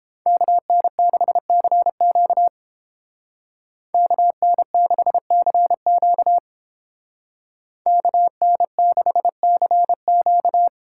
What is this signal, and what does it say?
Morse code, heard as a keyed tone, about 30 wpm, 700 Hz. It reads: KN6CQ KN6CQ KN6CQ